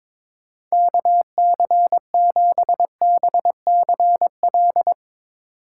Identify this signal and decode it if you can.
Morse code, heard as a keyed tone, about 22 wpm, 700 Hz. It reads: KC7BCL
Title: USA Call Signs - 3x Character Spacing - 22wpm